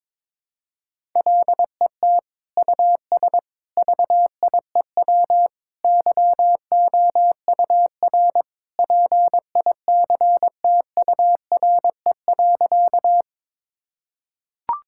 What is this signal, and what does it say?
Morse code, heard as a keyed tone, about 22 wpm, 700 Hz. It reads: LET US VIEW YOUR PICTURE.
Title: Sentences from Top 1000 Words - Review Part 6 - 22wpm